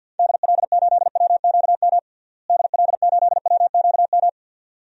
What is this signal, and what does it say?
Morse code, heard as a keyed tone, about 47 wpm, 700 Hz. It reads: DL8WQM DL8WQM
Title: DXCC-100 Call Signs - 6 Characters - Repeated 2x - 50wpm